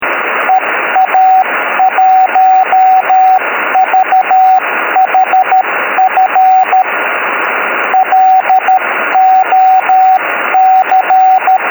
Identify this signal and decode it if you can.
Morse code, heard as a keyed tone, about 13 wpm, 750 Hz, with radio noise. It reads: EA1VHF LOC